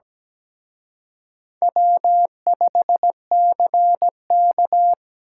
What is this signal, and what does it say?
Morse code, heard as a keyed tone, about 20 wpm, 700 Hz. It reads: W5CK
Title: Speed Racing - USA 1x2 Call Signs 17wpm